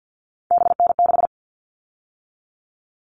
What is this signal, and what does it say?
Morse code, heard as a keyed tone, about 50 wpm, 700 Hz. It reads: BN6
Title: All Cut Numbers - 3-Digit 50wpm